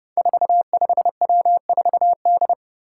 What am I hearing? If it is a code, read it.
Morse code, heard as a keyed tone, about 30 wpm, 700 Hz. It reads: 45W4D
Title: Five Challenging Letter-Number Combinations 30wpm